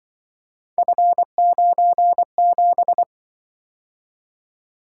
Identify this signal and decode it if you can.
Morse code, heard as a keyed tone, about 25 wpm, 700 Hz. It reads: F97
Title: Speed-Racing - Three Letter-Number Combinations - 20wpm